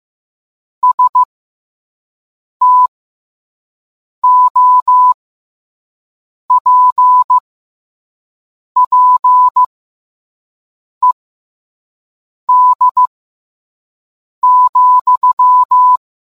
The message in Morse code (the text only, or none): STOPPED,